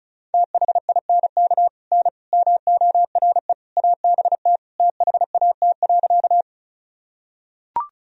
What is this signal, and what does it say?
Morse code, heard as a keyed tone, about 36 wpm, 700 Hz. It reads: THINK N MORE ABT THAT.
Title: Sentences from Top 200 Words and Abbreviations 35wpm